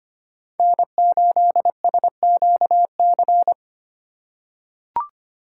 Morse code, heard as a keyed tone, about 25 wpm, 700 Hz. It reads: N8SQC
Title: USA Call Signs - 5 Characters - 2x Character Spacing - 25wpm